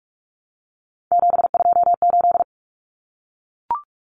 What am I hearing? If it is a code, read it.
Morse code, heard as a keyed tone, about 44 wpm, 700 Hz. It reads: 728
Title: Common Cut Numbers - 3-Digit 45wpm